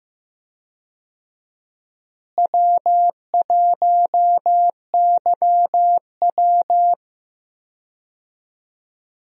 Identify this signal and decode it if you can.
Morse code, heard as a keyed tone, about 15 wpm, 700 Hz. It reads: W1YW
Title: USA Call Signs - 4 Characters - Repeated 2x - 15wpm